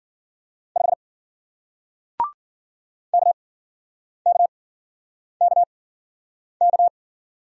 Morse code, heard as a keyed tone, about 50 wpm, spaced as for 9 wpm, 700 Hz. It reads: 5 KKKK